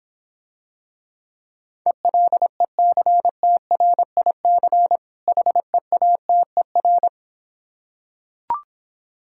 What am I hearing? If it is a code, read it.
Morse code, heard as a keyed tone, about 26 wpm, 700 Hz. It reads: ELECTRIC HEATER